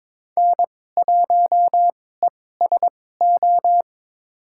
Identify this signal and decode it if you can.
Morse code, heard as a keyed tone, about 22 wpm, spaced as for 16 wpm, 700 Hz. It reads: N1ESO
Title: USA Call Signs - 5 Characters - 2x Character Spacing - 22wpm